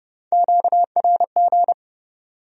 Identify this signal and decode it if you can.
Morse code, heard as a keyed tone, about 29 wpm, 700 Hz. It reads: QRG